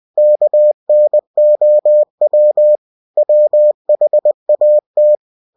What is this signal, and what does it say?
Morse code, heard as a keyed tone, about 20 wpm, 600 Hz. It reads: KNOW WHAT